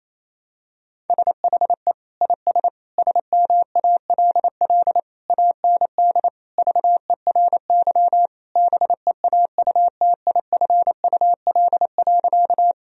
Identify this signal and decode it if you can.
Morse code, heard as a keyed tone, about 28 wpm, 700 Hz. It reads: SHE IS SMALL AND VERY BEAUTIFUL.
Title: Sentences from Top 1000 Words - Review Part 1 - 28wpm